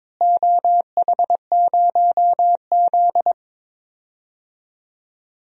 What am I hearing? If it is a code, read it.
Morse code, heard as a keyed tone, about 22 wpm, 700 Hz. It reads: OH0Z
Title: DXCC-100 Call Signs - 4 Characters - Repeated 2x - 22wpm